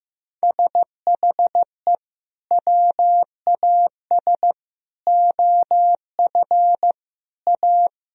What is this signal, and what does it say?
Morse code, heard as a keyed tone, about 16 wpm, 700 Hz. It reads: SHE WAS OF A
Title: Sentences from Top 1000 Words - Review Part 2 - 15wpm